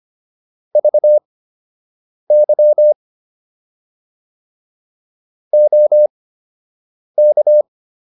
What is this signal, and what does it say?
Morse code, heard as a keyed tone, about 25 wpm, spaced as for 7 wpm, 600 Hz. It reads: VY OK